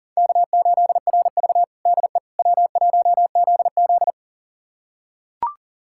Morse code, heard as a keyed tone, about 40 wpm, 700 Hz. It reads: K8RV DE W1ZZ
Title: USA Callsign DE Callsign - 4 Characters - 40wpm